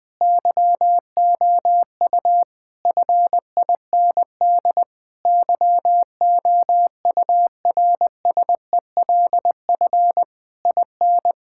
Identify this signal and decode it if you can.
Morse code, heard as a keyed tone, about 20 wpm, 700 Hz. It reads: YOU FIND YOURSELF IN
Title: Sentences from Top 1000 Words - Review Part 5 - 20wpm